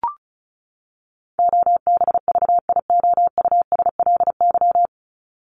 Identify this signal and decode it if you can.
Morse code, heard as a keyed tone, about 34 wpm, 700 Hz. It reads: OBVIOUSLY
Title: LY Suffix - Top 100 Words with Base Words 35wpm